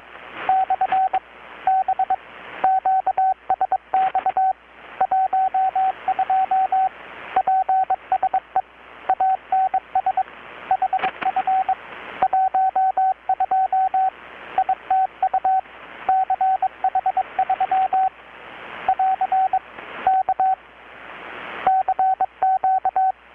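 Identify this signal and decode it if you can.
Morse code, heard as a keyed tone, about 21 wpm, 760 Hz, with radio noise. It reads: / B QSX 12 PSE ANS HF 12 ITU CH3 <AR> K CQ
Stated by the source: Morse from KPH's Night of Nights XX maritime radiotelegraphy event, received in Chiapas, Mexico about 3,800 km away on a Kenwood R-600 with a grounded 10 m long-wire antenna.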